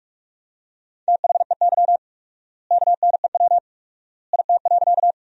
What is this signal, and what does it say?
Morse code, heard as a keyed tone, about 46 wpm, 700 Hz. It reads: THEY KNEW IT.
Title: 4x Word-Spacing - Sentences from Top 400 Words - 45wpm